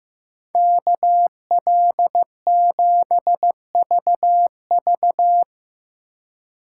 Morse code, heard as a keyed tone, about 15 wpm, 700 Hz. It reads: KL7VV
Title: USA Call Signs - 5 Characters - 2x Character Spacing - 15wpm